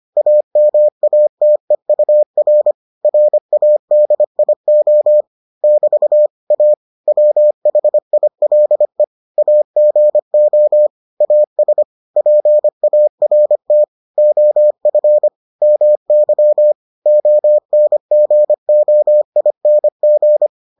Morse code, heard as a keyed tone, about 25 wpm, 600 Hz. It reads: AMATEUR RADIO <BT> A WHILE AGO AS PART OF MY ONGOING